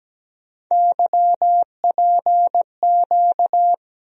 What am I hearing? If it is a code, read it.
Morse code, heard as a keyed tone, about 17 wpm, 700 Hz. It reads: YPQ
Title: USA Call Sign Postfixes - 3 Letters 17wpm